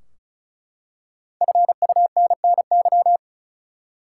Morse code, heard as a keyed tone, about 34 wpm, 700 Hz. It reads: FUNNY
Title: Five Letter Words - 35wpm